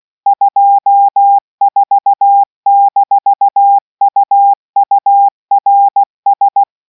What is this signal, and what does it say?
Morse code, heard as a keyed tone, about 16 wpm, 800 Hz. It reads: 24-UURS